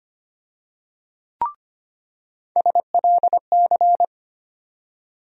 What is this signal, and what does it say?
Morse code, heard as a keyed tone, about 25 wpm, 700 Hz. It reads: SLC
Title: USA Call Sign Postfixes - 3 Letters 25wpm